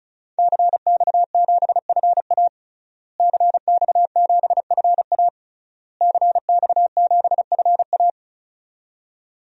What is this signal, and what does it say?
Morse code, heard as a keyed tone, about 34 wpm, 700 Hz. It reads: CX7FA CX7FA CX7FA
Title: DXCC-100 Call Signs - 5 Characters - Repeated 3x - 35wpm